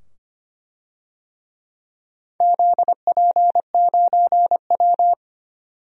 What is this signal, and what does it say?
Morse code, heard as a keyed tone, about 25 wpm, 700 Hz. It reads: ZP9W
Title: DXCC-100 Call Signs - 4 Characters - 25wpm